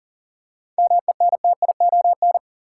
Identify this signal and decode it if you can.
Morse code, heard as a keyed tone, about 40 wpm, 700 Hz. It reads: MENTION
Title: Seven Letter Words - 40wpm